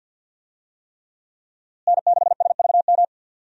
Morse code, heard as a keyed tone, about 46 wpm, 700 Hz. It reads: N6SVM
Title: USA Call Signs - 5 Characters - Repeated 3x - 50wpm